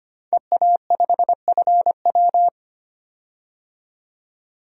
Morse code, heard as a keyed tone, about 25 wpm, 700 Hz. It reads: EA5FW
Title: DXCC-100 Call Signs - 5 Characters - 25wpm